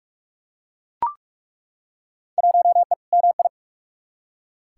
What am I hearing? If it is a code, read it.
Morse code, heard as a keyed tone, about 42 wpm, 700 Hz. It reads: 1E MI
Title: Field Day Exchanges - 45wpm